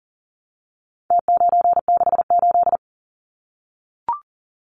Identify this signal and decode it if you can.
Morse code, heard as a keyed tone, about 40 wpm, 700 Hz. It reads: T968
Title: Numbers - Contest Style - 4-Digit 40wpm